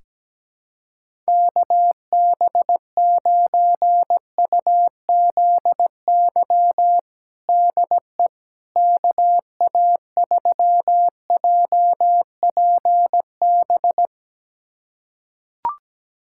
Morse code, heard as a keyed tone, about 17 wpm, 700 Hz. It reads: KB9UZY DE KA3JPB
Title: USA Callsign DE Callsign - 6 Characters - 17wpm